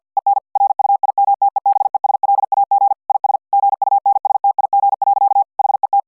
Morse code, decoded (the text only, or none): EN RUIMTEBESLAG IS GUNSTIG. HET